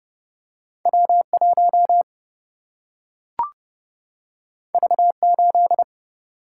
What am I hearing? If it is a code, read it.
Morse code, heard as a keyed tone, about 30 wpm, 700 Hz. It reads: W 1 V 8
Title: Single-Letter Single-Number 30wpm